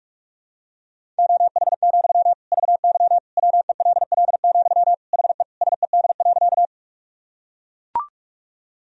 Morse code, heard as a keyed tone, about 42 wpm, 690 Hz. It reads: OH, VY WELL, HE SED.
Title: Sentences from Top 500 Words and Abbreviations 45wpm